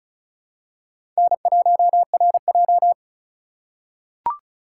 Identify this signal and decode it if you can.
Morse code, heard as a keyed tone, about 36 wpm, 700 Hz. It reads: N1RJ
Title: USA Call Signs - 4x Character Spacing - 35wpm